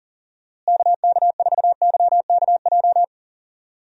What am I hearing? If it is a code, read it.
Morse code, heard as a keyed tone, about 40 wpm, 700 Hz. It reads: KK4YKJ
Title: USA Call Signs - 6 Characters - Repeated 2x - 40wpm